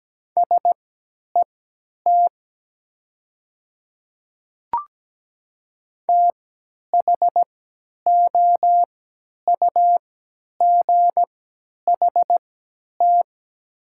Morse code, heard as a keyed tone, about 17 wpm, spaced as for 10 wpm, 700 Hz. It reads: SET THOUGHT
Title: 3x Character Spacing - Top 300 Words - 17wpm